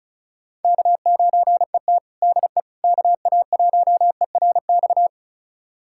Morse code, heard as a keyed tone, about 34 wpm, 700 Hz. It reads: K9ET DE KA1ERX
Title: USA Callsign DE Callsign 35wpm